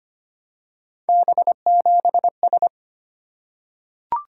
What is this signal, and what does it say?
Morse code, heard as a keyed tone, about 25 wpm, 700 Hz. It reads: B7S
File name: Three Letter-Number Combinations - 25wpm